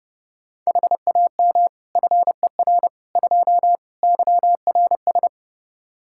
Morse code, heard as a keyed tone, about 32 wpm, 700 Hz. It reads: HAM FER 2 YRS
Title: Practice QSO - Exchange 3 - More Info - 30wpm